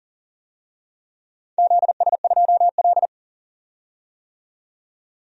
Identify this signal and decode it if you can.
Morse code, heard as a keyed tone, about 42 wpm, 700 Hz. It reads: ZS2L
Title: DXCC-100 Call Signs - 4 Characters - 40wpm